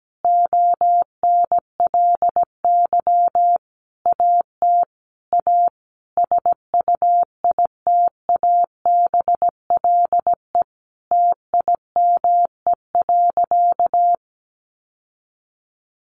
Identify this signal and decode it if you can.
Morse code, heard as a keyed tone, about 17 wpm, 700 Hz. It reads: ONLY AT A SUITABLE TIME.
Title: Sentences from Top 1400 Words 17wpm